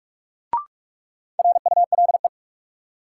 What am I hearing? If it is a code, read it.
Morse code, heard as a keyed tone, about 46 wpm, 700 Hz. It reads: RULE
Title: Sentences from Top 400 Words - New Word Drill - 45wpm